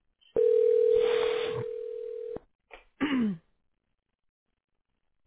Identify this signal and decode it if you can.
Ringback tone (North America), 440 + 480 Hz.